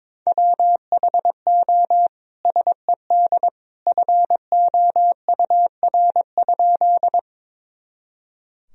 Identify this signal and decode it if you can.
Morse code, heard as a keyed tone, about 22 wpm, 700 Hz. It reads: WHO SED FOUR?
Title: Sentences from Top 300 Words and Abbreviations 22wpm